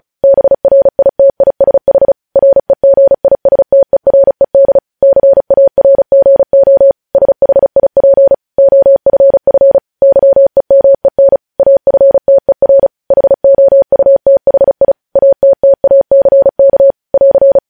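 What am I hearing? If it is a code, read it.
Morse code, heard as a keyed tone, about 36 wpm, 550 Hz. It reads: BRITISH REGISTERED CARGO SHIP OFF YEMEN AFTER HOUTHI ATTACK <AR>
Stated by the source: Morse code news headlines sent at 35 words per minute.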